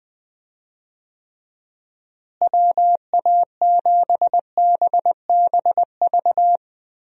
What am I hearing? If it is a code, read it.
Morse code, heard as a keyed tone, about 21 wpm, 700 Hz. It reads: WA7BBV
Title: USA Call Signs - 6 Characters - Repeated 3x - 20wpm